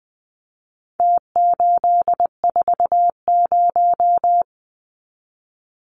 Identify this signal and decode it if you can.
Morse code, heard as a keyed tone, about 20 wpm, 700 Hz. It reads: T840
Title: Numbers - Contest Style - 4-Digit 20wpm